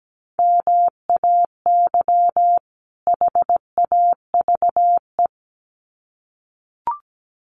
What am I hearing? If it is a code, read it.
Morse code, heard as a keyed tone, about 17 wpm, 700 Hz. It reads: MAY HAVE
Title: Sets of 2 Words 17wpm